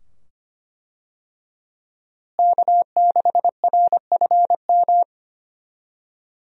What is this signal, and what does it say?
Morse code, heard as a keyed tone, about 25 wpm, 700 Hz. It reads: K6RFM
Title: USA Call Signs - 5 Characters - Repeated 2x - 25wpm